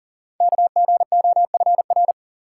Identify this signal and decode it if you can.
Morse code, heard as a keyed tone, about 41 wpm, 700 Hz. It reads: KGOFR
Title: Code Groups - 5 Letters - 40wpm